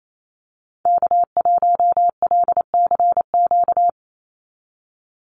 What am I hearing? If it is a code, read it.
Morse code, heard as a keyed tone, about 28 wpm, 700 Hz. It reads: K1LCQ
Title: USA Call Signs 28wpm